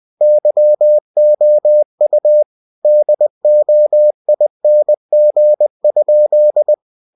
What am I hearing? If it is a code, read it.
Morse code, heard as a keyed tone, about 20 wpm, 600 Hz. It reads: YOU DOING?
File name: cw-books / winnie-the-pooh / winnie-20wpm-page-0025